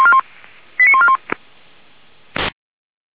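Five-tone selective call (ZVEI), heard as quick runs of five tones, tones of about 80 ms. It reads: 89141
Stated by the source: ZVEI five-tone call 89155 received by SDR.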